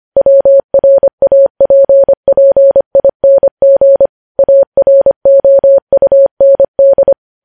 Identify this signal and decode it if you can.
Morse code, heard as a keyed tone, about 25 wpm, 550 Hz. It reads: WRAPPING AROUND
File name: podcast / News headlines-1x-25x25